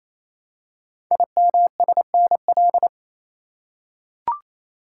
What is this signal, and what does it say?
Morse code, heard as a keyed tone, about 28 wpm, 700 Hz. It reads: IMSNL
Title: Code Groups - 5 Letters - 28wpm